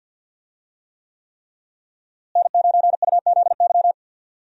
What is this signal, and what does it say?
Morse code, heard as a keyed tone, about 52 wpm, 700 Hz. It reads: N9UZY
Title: USA Call Signs - 5 Characters - Repeated 3x - 50wpm